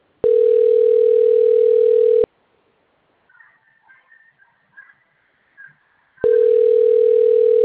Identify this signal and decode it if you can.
Ringback tone (North America), 440 + 480 Hz.